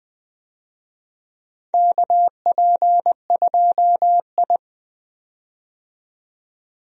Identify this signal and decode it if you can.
Morse code, heard as a keyed tone, about 20 wpm, 700 Hz. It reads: KP2I
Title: DXCC-100 Call Signs - 4 Characters - 20wpm